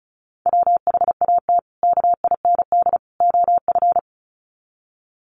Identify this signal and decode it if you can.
Morse code, heard as a keyed tone, about 34 wpm, 700 Hz. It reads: WHAT KIND OF
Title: Sets of 3 Words 35wpm